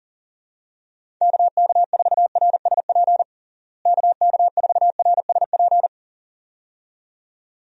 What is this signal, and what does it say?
Morse code, heard as a keyed tone, about 40 wpm, 700 Hz. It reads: KK4RSP KK4RSP
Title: USA Call Signs - 6 Characters - Repeated 2x - 40wpm